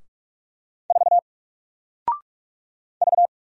Morse code, heard as a keyed tone, about 46 wpm, 700 Hz. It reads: 4 V
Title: Single Character (Letter:Number:Punctuation:Prosign) Alt 45wpm